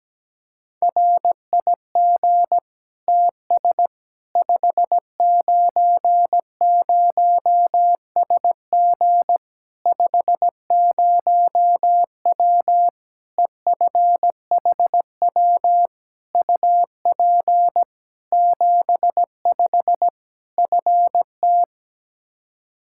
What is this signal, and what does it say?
Morse code, heard as a keyed tone, about 17 wpm, 700 Hz. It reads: RIG TS 590SG 50W EFHW UP 75 FT
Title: Practice QSO - Exchange 2 - Info - 17wpm